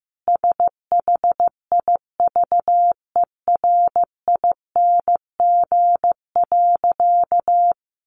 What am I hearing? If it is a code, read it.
Morse code, heard as a keyed tone, about 15 wpm, 700 Hz. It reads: SHIVERING.
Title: A Sound Of Thunder 15wpm